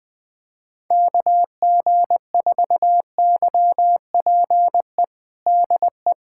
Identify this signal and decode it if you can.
Morse code, heard as a keyed tone, about 20 wpm, 700 Hz. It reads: KG4YPE DE
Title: USA Callsign DE Callsign 20wpm